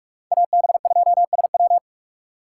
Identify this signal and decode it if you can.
Morse code, heard as a keyed tone, about 46 wpm, 690 Hz. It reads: AB2SW
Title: USA Call Signs - 5 Characters - 6x Character Spacing - 45wpm